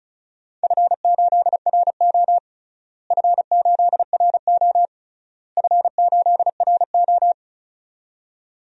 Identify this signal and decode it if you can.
Morse code, heard as a keyed tone, about 36 wpm, 700 Hz. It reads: F8RO F8RO F8RO
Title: DXCC-100 Call Signs - 4 Characters - Repeated 3x - 35wpm